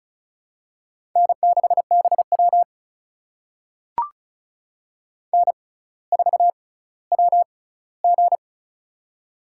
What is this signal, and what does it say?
Morse code, heard as a keyed tone, about 36 wpm, spaced as for 18 wpm, 700 Hz. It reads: N6BW N4WG
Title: USA Call Signs - 4 Characters - 6x Character Spacing - 35wpm